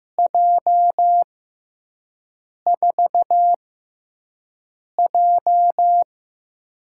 Morse code, heard as a keyed tone, about 15 wpm, spaced as for 5 wpm, 700 Hz. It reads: J4J
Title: USA Call Signs - 4 Characters - 6x Character Spacing - 15wpm